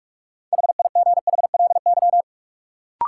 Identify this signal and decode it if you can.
Morse code, heard as a keyed tone, about 46 wpm, 700 Hz. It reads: HIGHLY
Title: Sentences from Top 1100 Words - New Word Drill Alt - 45wpm